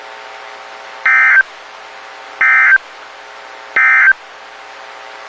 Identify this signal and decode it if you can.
Emergency Alert System (SAME) tones, heard as harsh buzzing bursts. It is heard from 0.9 to 4.2 s.
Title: WXK91 NOAA Weather Radio